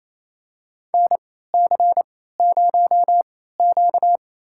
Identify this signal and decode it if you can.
Morse code, heard as a keyed tone, about 27 wpm, spaced as for 16 wpm, 700 Hz. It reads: NC0Q
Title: USA Call Signs - 3x Character Spacing - 28wpm